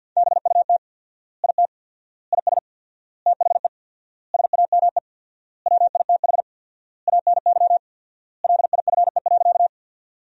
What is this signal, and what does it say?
Morse code, heard as a keyed tone, about 50 wpm, 700 Hz. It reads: BUT IT IS THE SAME WITH ANY LIFE.
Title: 4x Word-Spacing - Sentences from Top 300 Words - 50wpm